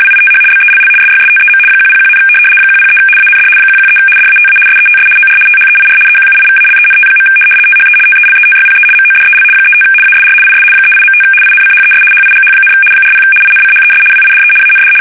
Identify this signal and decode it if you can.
RTTY, about 75 baud. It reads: WITH BELL SYSTEM 101, 103 AND 113 DATA SETS OR THEIR EQUIVALENT. THE TELETYPE MODEL 35 ASR IS 38.5 INCHES HIGH, 40 INCHES WIDE AND
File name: fsk850-75B / teletype-corp